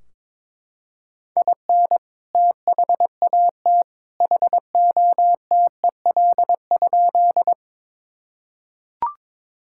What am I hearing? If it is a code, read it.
Morse code, heard as a keyed tone, about 22 wpm, 700 Hz. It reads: IN THAT HOTEL?
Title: Sentences from Top 1600 Words 22wpm